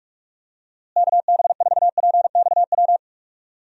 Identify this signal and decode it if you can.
Morse code, heard as a keyed tone, about 43 wpm, 700 Hz. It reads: KB4PXW
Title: USA Call Signs - 6 Characters - Repeated 2x - 45wpm